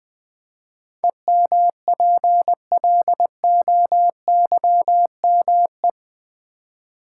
Morse code, heard as a keyed tone, about 20 wpm, 700 Hz. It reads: EMPLOYME
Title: Build A Word - 4 Letters At A Time - 10-Letter Words - 20wpm